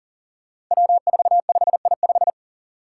Morse code, heard as a keyed tone, about 38 wpm, 700 Hz. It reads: W45I5
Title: Five Challenging Letter-Number Combinations 40wpm